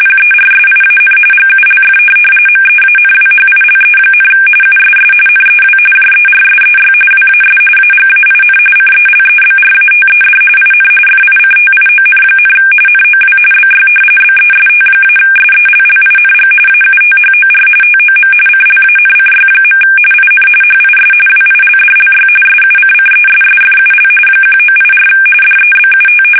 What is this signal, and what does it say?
RTTY, about 45.45 baud. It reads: PERSIAN GULF MEANWHILE, USS NEW JERSEY WAS AGAIN CELEBRATED IN FICTION, NOW BY BEST-SELLING AUTHOR TOM CLANCY. IN HIS WILDLY SUCCESSFUL 1984 DEBUT
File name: fsk850 / battleshipnj